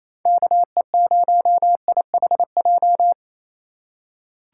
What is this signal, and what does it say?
Morse code, heard as a keyed tone, about 28 wpm, 700 Hz. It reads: KE0IHJ
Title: USA Call Signs - 6 Characters - 4x Character Spacing - 28wpm